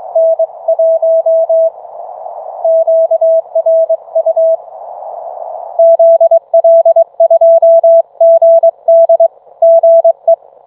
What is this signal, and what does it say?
Morse code, heard as a keyed tone, about 24 wpm, 660 Hz. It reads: N 1 QRU ZL2GD GE